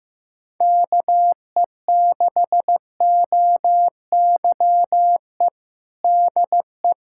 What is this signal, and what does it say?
Morse code, heard as a keyed tone, about 15 wpm, 700 Hz. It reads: KE6OYE DE
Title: Practice QSO - Exchange 4 - Ending - 15wpm